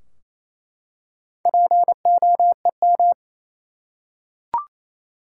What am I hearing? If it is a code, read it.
Morse code, heard as a keyed tone, about 28 wpm, 700 Hz. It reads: POEM
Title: Four Letter Words - 28wpm